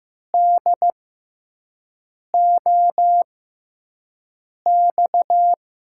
Morse code, heard as a keyed tone, about 15 wpm, spaced as for 5 wpm, 700 Hz. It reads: DOX
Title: USA Call Signs - 6 Characters - 6x Character Spacing - 15wpm